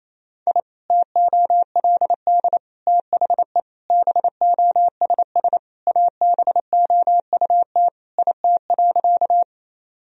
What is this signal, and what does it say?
Morse code, heard as a keyed tone, about 28 wpm, 700 Hz. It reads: I TOLD THE BOSS ABOUT IT.
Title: Sentences from Top 1000 Words - Review Part 4 - 28wpm